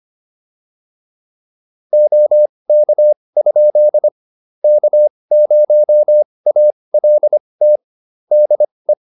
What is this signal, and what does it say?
Morse code, heard as a keyed tone, about 25 wpm, spaced as for 20 wpm, 600 Hz. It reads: OK? K0ALT DE